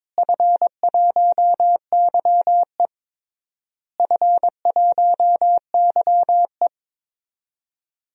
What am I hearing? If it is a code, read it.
Morse code, heard as a keyed tone, about 22 wpm, 700 Hz. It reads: F1YE F1YE
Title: DXCC-100 Call Signs - 4 Characters - Repeated 3x - 22wpm